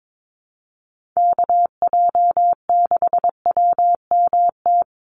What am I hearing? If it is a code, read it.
Morse code, heard as a keyed tone, about 22 wpm, 700 Hz. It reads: KJ6WMT
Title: USA Call Signs 22wpm